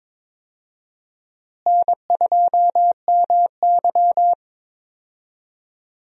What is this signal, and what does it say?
Morse code, heard as a keyed tone, about 22 wpm, 700 Hz. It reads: N2MY
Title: USA Call Signs - 4 Characters - Repeated 2x - 22wpm